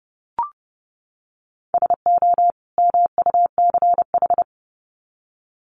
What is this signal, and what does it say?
Morse code, heard as a keyed tone, about 30 wpm, 700 Hz. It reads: SO MUCH
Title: Sets of 2 Words 30wpm